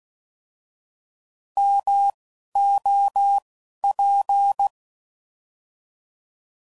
Morse code, heard as a keyed tone, about 16 wpm, 800 Hz. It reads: M O P